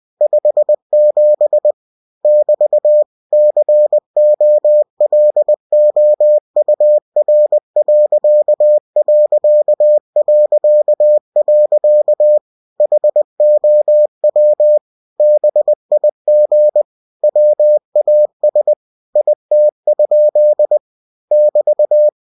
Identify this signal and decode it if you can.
Morse code, heard as a keyed tone, about 20 wpm, 600 Hz. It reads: 57 <BT> COLOUR.... HOW BIG WAS IT? <BT>